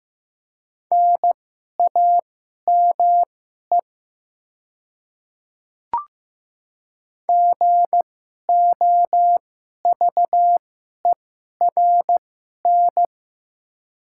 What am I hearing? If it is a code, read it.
Morse code, heard as a keyed tone, about 15 wpm, spaced as for 11 wpm, 690 Hz. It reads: NAME GOVERN
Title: 2x Character Spacing - Top 500 Words - 15wpm